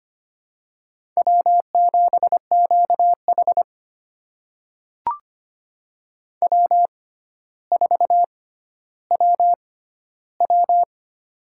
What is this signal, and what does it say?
Morse code, heard as a keyed tone, about 25 wpm, spaced as for 13 wpm, 700 Hz. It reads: W7QH W4WW
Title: USA Call Signs - 4 Characters - 6x Character Spacing - 25wpm